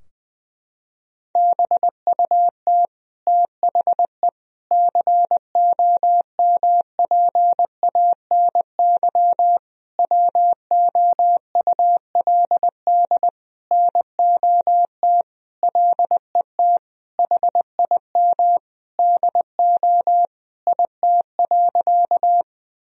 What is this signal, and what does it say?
Morse code, heard as a keyed tone, about 20 wpm, 700 Hz. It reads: BUT THE COMPANY WOULD NOT LET HIM DO IT.
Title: Sentences from Top 1000 Words - Review Part 3 - 20wpm